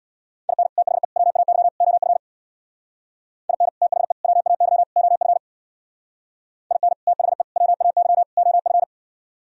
Morse code, heard as a keyed tone, about 63 wpm, 700 Hz. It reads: IN THE WAY OF IN THE WAY OF IN THE WAY OF